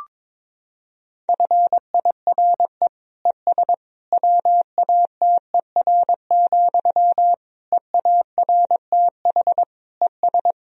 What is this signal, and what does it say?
Morse code, heard as a keyed tone, about 22 wpm, 700 Hz. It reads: FIRE ES WATER, EARTH ES
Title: Sentences from Top 400 Words and Abbreviations 22wpm